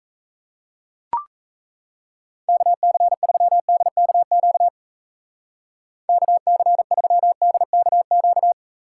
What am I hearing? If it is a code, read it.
Morse code, heard as a keyed tone, about 35 wpm, 700 Hz. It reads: KC3DKQ KC3DKQ